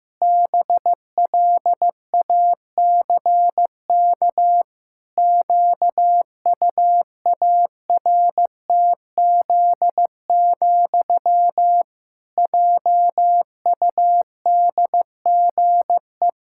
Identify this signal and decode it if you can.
Morse code, heard as a keyed tone, about 16 wpm, 700 Hz. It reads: BLACK QUARTZ, JUDGE
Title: Pangram Challenge v2 - 15wpm